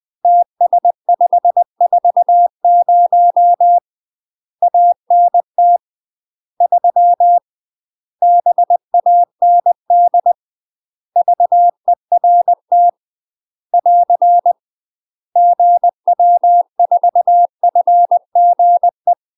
Morse code, heard as a keyed tone, about 20 wpm, 700 Hz. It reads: TS540 ANT 3 BAND VERT <AR> GW4FGE
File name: UK MASTER QSOs TRAINING / 20WPM / UK MASTER QSOs 0023